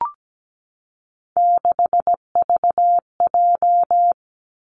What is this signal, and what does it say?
Morse code, heard as a keyed tone, about 17 wpm, 700 Hz. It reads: 6VJ
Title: Three Challenging Letter-Number Combinations 17wpm